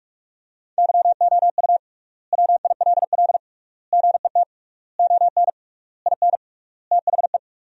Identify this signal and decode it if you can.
Morse code, heard as a keyed tone, about 41 wpm, 700 Hz. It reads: YOU WILL GET ON IN THE
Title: 3x Word-Spacing - Sentences from Top 200 Words - 45wpm